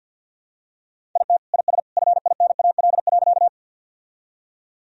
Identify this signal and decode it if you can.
Morse code, heard as a keyed tone, about 52 wpm, 700 Hz. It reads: IT IS FINAL.
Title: Sentences from Top 1000 Words - Review Part 2 - 50wpm